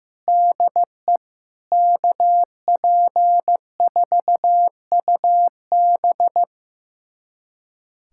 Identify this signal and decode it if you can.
Morse code, heard as a keyed tone, about 16 wpm, 700 Hz. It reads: DE KP4UB
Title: USA Callsign DE Callsign 15wpm